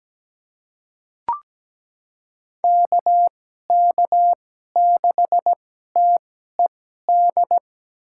Morse code, heard as a keyed tone, about 17 wpm, spaced as for 12 wpm, 700 Hz. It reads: KK6TED
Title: USA Call Signs - 6 Characters - 2x Character Spacing - 17wpm